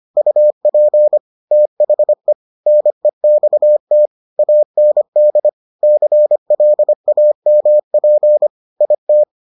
Morse code, heard as a keyed tone, about 25 wpm, 600 Hz. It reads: UP THE NEXT AND CLAMP IT